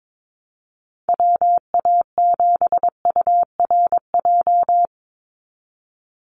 Morse code, heard as a keyed tone, about 22 wpm, 700 Hz. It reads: WA7URJ
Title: USA Call Signs 22wpm